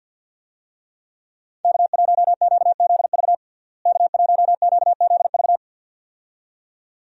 Morse code, heard as a keyed tone, about 49 wpm, 700 Hz. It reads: K1QZV K1QZV
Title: USA Call Signs - 5 Characters - Repeated 2x - 50wpm